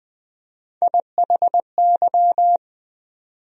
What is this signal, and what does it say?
Morse code, heard as a keyed tone, about 20 wpm, 700 Hz. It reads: IHY
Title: USA Call Sign Postfixes - 3 Letters 20wpm